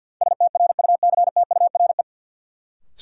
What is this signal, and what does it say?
Morse code, heard as a keyed tone, about 52 wpm, 700 Hz. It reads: STRUCTURE
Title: Sentences from Top 1000 Words - Review Part 6 - New Word Drill - 50wpm